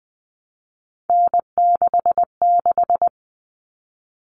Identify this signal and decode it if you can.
Morse code, heard as a keyed tone, about 20 wpm, 700 Hz. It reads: N66
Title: All Cut Numbers - 3-Digit 20wpm